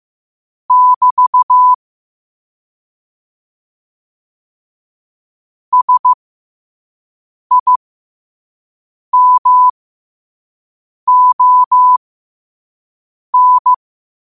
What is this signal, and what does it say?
Morse code, heard as a keyed tone, about 15 wpm, spaced as for 5 wpm, 1000 Hz. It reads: <BT> SIMON
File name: qotdfiles / 5wpm